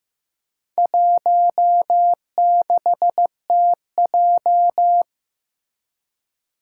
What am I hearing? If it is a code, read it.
Morse code, heard as a keyed tone, about 15 wpm, 700 Hz. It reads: 16TJ
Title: Four Challenging Letter-Number Combinations 15wpm